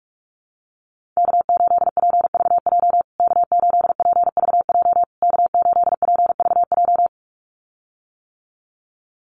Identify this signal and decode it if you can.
Morse code, heard as a keyed tone, about 44 wpm, 700 Hz. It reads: K8PVJ K8PVJ K8PVJ
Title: USA Call Signs Repeated 3x 45wpm